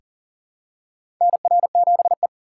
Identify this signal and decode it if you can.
Morse code, heard as a keyed tone, about 40 wpm, 700 Hz. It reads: NR7E
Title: USA Call Signs - 3x Character Spacing - 40wpm